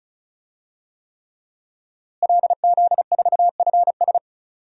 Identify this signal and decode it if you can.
Morse code, heard as a keyed tone, about 34 wpm, 700 Hz. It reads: LZ4FS
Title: DXCC-100 Call Signs - 5 Characters - Repeated 3x - 35wpm